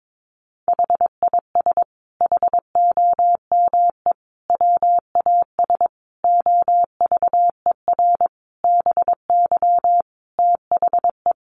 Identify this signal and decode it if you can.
Morse code, heard as a keyed tone, about 22 wpm, 700 Hz. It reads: HIS HOME WAS OVER BY THE
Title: Sentences from Top 1500 Words 22wpm